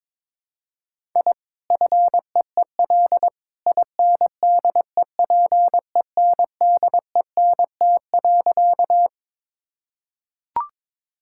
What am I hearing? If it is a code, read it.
Morse code, heard as a keyed tone, about 22 wpm, 700 Hz. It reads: I FEEL INDEPENDENT.
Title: Sentences from Top 1000 Words - Review Part 6 - 22wpm